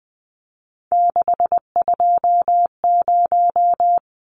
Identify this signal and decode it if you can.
Morse code, heard as a keyed tone, about 20 wpm, 700 Hz. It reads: 620
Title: Numbers - 3-Digit 20wpm